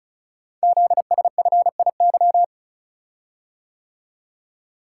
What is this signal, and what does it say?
Morse code, heard as a keyed tone, about 36 wpm, 700 Hz. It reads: ZSFIY
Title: Code Groups - 5 Letters - 35wpm